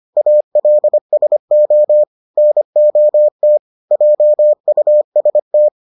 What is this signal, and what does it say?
Morse code, heard as a keyed tone, about 25 wpm, 600 Hz. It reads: ALSO NOT JUST